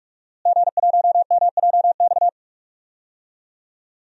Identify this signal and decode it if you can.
Morse code, heard as a keyed tone, about 42 wpm, 700 Hz. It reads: G1MJX
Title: DXCC-100 Call Signs - 5 Characters - Repeated 3x - 45wpm